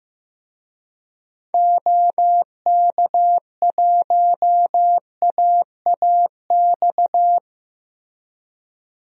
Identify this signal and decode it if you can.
Morse code, heard as a keyed tone, about 15 wpm, 700 Hz. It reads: OK1AAX
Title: DXCC-100 Call Signs - 6 Characters - Repeated 2x - 15wpm